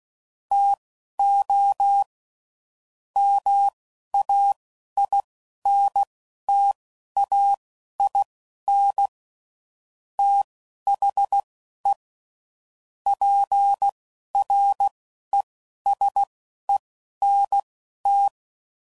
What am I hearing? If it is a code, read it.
Morse code, heard as a keyed tone, about 16 wpm, spaced as for 11 wpm, 800 Hz. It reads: TO MAINTAIN THE PRESENT
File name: CW Code Groups @10WPM / Technical Subjects / XL